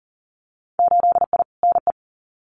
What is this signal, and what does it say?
Morse code, heard as a keyed tone, about 40 wpm, 700 Hz. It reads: 8I NE
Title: Winter Field Day Exchanges Encore 40wpm